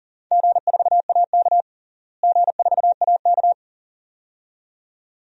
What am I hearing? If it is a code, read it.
Morse code, heard as a keyed tone, about 40 wpm, 700 Hz. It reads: G4AK G4AK
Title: DXCC-100 Call Signs - 4 Characters - Repeated 2x - 40wpm